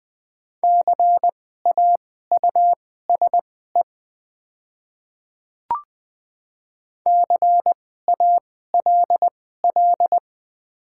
Morse code, heard as a keyed tone, about 21 wpm, spaced as for 15 wpm, 700 Hz. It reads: CAUSE CALL
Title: 2x Character Spacing - Top 300 Words - 20wpm